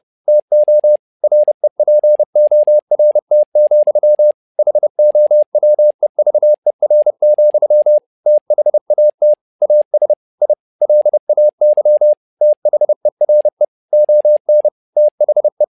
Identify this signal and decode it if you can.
Morse code, heard as a keyed tone, about 30 wpm, 600 Hz. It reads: TO REPORT, HOWEVER, THAT AS I LAY THERE ON THE
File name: learning center / tower trouble / tower trouble 30wpm